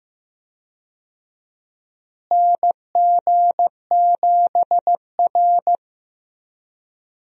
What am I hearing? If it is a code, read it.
Morse code, heard as a keyed tone, about 15 wpm, 700 Hz. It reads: NG7R
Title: USA Call Signs - 4 Characters - Repeated 2x - 15wpm